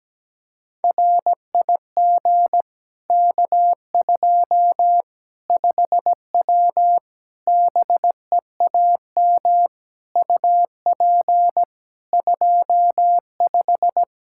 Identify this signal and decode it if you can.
Morse code, heard as a keyed tone, about 17 wpm, 700 Hz. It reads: RIG K2 5W BEAM UP 25
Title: Practice QSO - Exchange 2 - Info - 17wpm